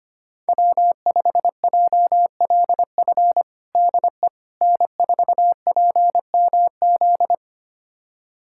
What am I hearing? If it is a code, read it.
Morse code, heard as a keyed tone, about 25 wpm, 700 Hz. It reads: W5JLF DE N4PMZ
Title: USA Callsign DE Callsign - 5 Characters - 25wpm